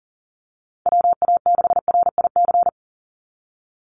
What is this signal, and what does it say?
Morse code, heard as a keyed tone, about 40 wpm, 700 Hz. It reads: WA6RIC
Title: NA SOTA Chaser Callsigns 40wpm